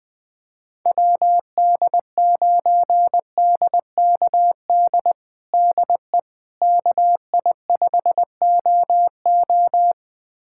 Morse code, heard as a keyed tone, about 20 wpm, 700 Hz. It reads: WD9DKD DE KI5OO
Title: USA Callsign DE Callsign 20wpm